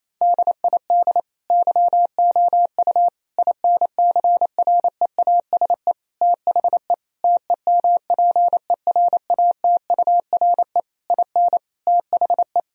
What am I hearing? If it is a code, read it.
Morse code, heard as a keyed tone, about 28 wpm, 700 Hz. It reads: DID YOU INCREASE THE TEMPERATURE IN THE
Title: Sentences from Top 1000 Words - Review Part 4 - 28wpm